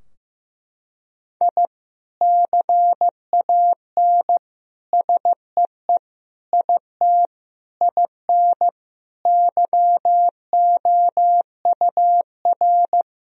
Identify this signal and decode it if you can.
Morse code, heard as a keyed tone, about 15 wpm, 700 Hz. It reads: I CAN SEE IT IN YOUR
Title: Sentences from Top 1000 Words - Review Part 2 - 15wpm